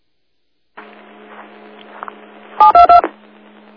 DTMF, heard as telephone touch tones, tones of about 120 ms. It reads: 722
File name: Old Forge Group 08 40 42